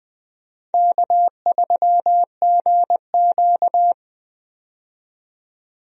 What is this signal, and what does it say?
Morse code, heard as a keyed tone, about 21 wpm, 700 Hz. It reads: K3GQ
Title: USA Call Signs - 4 Characters - 20wpm